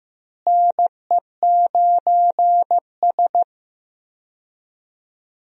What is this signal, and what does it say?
Morse code, heard as a keyed tone, about 15 wpm, 700 Hz. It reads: NE9S
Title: USA Call Signs - 4 Characters - Repeated 2x - 15wpm